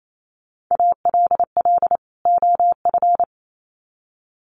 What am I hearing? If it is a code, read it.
Morse code, heard as a keyed tone, about 28 wpm, 700 Hz. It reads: ALL OF
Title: Sets of 2 Words 28wpm